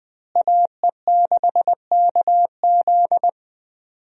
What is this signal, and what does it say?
Morse code, heard as a keyed tone, about 21 wpm, 700 Hz. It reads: AE6KZ
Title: USA Call Signs - 5 Characters - Repeated 2x - 20wpm